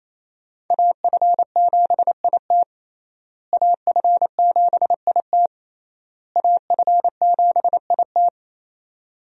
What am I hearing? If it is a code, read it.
Morse code, heard as a keyed tone, about 28 wpm, 700 Hz. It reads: AF7IT AF7IT AF7IT
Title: USA Call Signs - 5 Characters - Repeated 3x - 28wpm